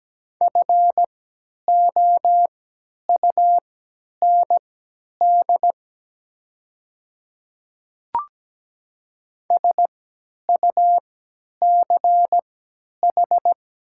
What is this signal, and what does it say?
Morse code, heard as a keyed tone, about 17 wpm, spaced as for 10 wpm, 700 Hz. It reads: FOUND SUCH
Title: 3x Character Spacing - Top 300 Words - 17wpm